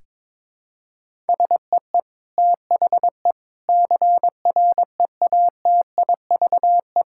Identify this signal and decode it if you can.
Morse code, heard as a keyed tone, about 21 wpm, 700 Hz. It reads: SEE THE CREATIVE
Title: Sentences from Top 1000 Words - Review Part 5 - 22wpm